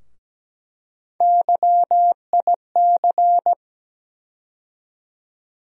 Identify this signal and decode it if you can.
Morse code, heard as a keyed tone, about 17 wpm, 700 Hz. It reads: YIC
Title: USA Call Sign Postfixes - 3 Letters 17wpm